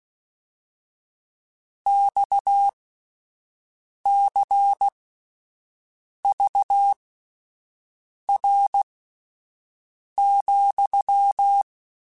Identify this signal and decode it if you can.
Morse code, heard as a keyed tone, about 16 wpm, spaced as for 5 wpm, 800 Hz. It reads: XCVR,
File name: QSOs / QSOs-mp3s / WD0FBT 1@10